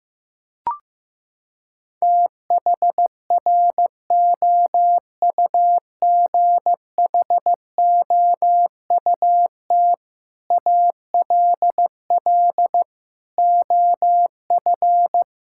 Morse code, heard as a keyed tone, about 15 wpm, 700 Hz. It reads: THROUGHOUT ALL OF
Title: Sentences from Top 1000 Words - Review Part 5 - 15wpm